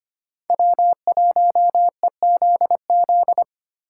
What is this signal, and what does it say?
Morse code, heard as a keyed tone, about 31 wpm, 700 Hz. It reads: W1EZZ
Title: Speed Racing - USA 1x3 Call Signs 25wpm